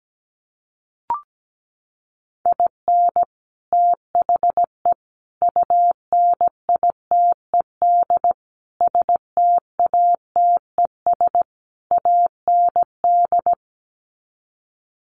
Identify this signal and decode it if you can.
Morse code, heard as a keyed tone, about 17 wpm, 700 Hz. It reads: IN THE UNITED STATES AND
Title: Sets of 5 Words 17wpm